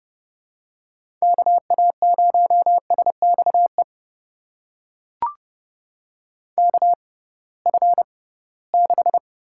Morse code, heard as a keyed tone, about 30 wpm, spaced as for 19 wpm, 700 Hz. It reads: KA0SXE KF6
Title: USA Call Signs - 6 Characters - 6x Character Spacing - 30wpm